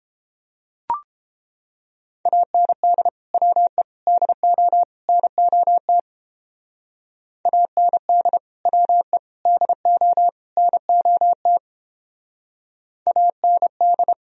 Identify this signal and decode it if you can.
Morse code, heard as a keyed tone, about 31 wpm, 700 Hz. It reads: AND WE DO NOT AND WE DO NOT AND